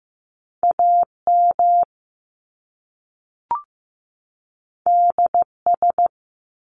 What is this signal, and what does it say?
Morse code, heard as a keyed tone, about 15 wpm, 700 Hz. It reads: AM DS
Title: Top Two Letter Combinations in Words 15wpm